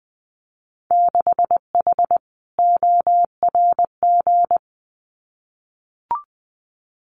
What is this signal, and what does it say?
Morse code, heard as a keyed tone, about 20 wpm, 700 Hz. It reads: 6H ORG
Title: Winter Field Day Exchanges Encore 20wpm